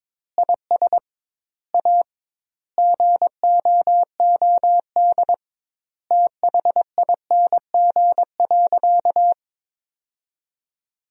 Morse code, heard as a keyed tone, about 22 wpm, 700 Hz. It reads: IS A GOOD THING.
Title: 2x Word-Spacing - Sentences from Top 300 Words - 22wpm